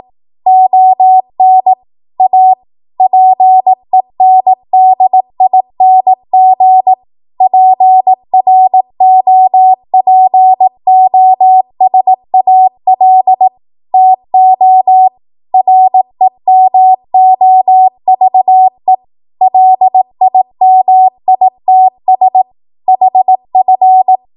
Morse code, heard as a keyed tone, about 18 wpm, 750 Hz. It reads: ON A PENDING PROPOSAL TO REMOVE LIMITS HF
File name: Morse / 18WPM